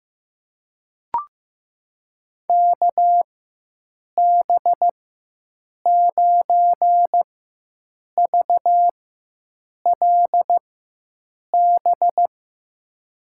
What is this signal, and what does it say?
Morse code, heard as a keyed tone, about 16 wpm, spaced as for 7 wpm, 700 Hz. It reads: KB9VLB
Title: USA Call Signs - 6 Characters - 4x Character Spacing - 15wpm